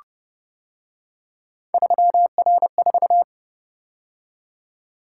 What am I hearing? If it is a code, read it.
Morse code, heard as a keyed tone, about 30 wpm, 700 Hz. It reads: 3R4